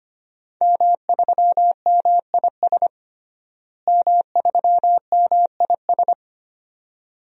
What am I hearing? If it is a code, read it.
Morse code, heard as a keyed tone, about 25 wpm, 700 Hz. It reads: M3MIS M3MIS
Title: DXCC-100 Call Signs - 5 Characters - Repeated 2x - 25wpm